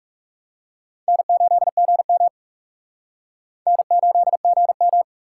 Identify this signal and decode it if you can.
Morse code, heard as a keyed tone, about 36 wpm, 700 Hz. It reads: N8GM N8GM
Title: Speed Racing - USA 1x2 Call Signs 30wpm